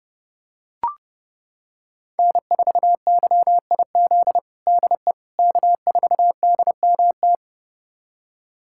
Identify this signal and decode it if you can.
Morse code, heard as a keyed tone, about 30 wpm, 700 Hz. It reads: N4YIZ DE K4DMT
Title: USA Callsign DE Callsign - 5 Characters - 30wpm